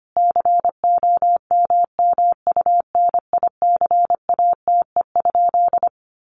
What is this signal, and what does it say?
Morse code, heard as a keyed tone, about 25 wpm, 700 Hz. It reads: COMMUNICATE?
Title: Sentences from Top 1300 Words 25wpm